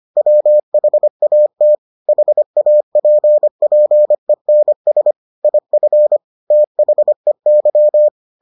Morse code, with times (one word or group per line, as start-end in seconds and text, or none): 0.2-1.8 WHAT
2.1-5.1 HAPPENS
5.4-6.2 IF
6.5-8.1 THEY